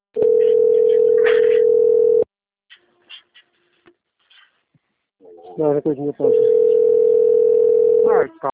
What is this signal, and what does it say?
Ringback tone (North America), 440 + 480 Hz, with speech.
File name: rg-1600-unknown- - - .23661